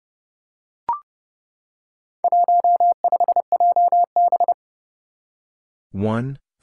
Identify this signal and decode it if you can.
Morse code, heard as a keyed tone, about 30 wpm, 700 Hz. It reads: 15JB
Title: Four Challenging Letter-Number Combinations 30wpm